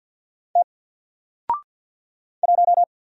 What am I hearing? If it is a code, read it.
Morse code, heard as a keyed tone, about 50 wpm, 700 Hz. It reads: T 1
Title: Single Character (Letter:Number:Punctuation:Prosign) 50wpm